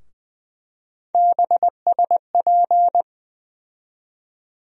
Morse code, heard as a keyed tone, about 20 wpm, 700 Hz. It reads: BSP
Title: USA Call Sign Postfixes - 3 Letters 20wpm